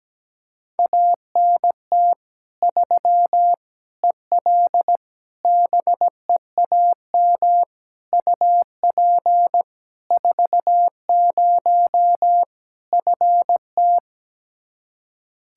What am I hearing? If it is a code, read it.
Morse code, heard as a keyed tone, about 17 wpm, 700 Hz. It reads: ANT 3 EL BEAM UP 40 FT
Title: K1USN SST QSO Practice 17wpm